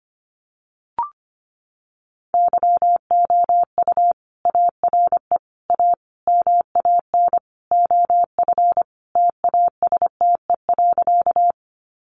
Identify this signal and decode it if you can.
Morse code, heard as a keyed tone, about 25 wpm, 700 Hz. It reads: YOU ARE A MAN OF TASTE.
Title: Sentences from Top 1100 Words 25wpm